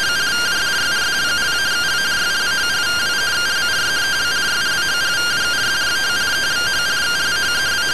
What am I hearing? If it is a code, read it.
Two-tone teleprinter signal, about 170 Hz shift.